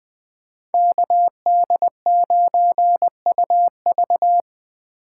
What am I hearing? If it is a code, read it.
Morse code, heard as a keyed tone, about 23 wpm, 700 Hz. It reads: KD9UV
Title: Speed Racing - USA 2x2 Call Signs 20wpm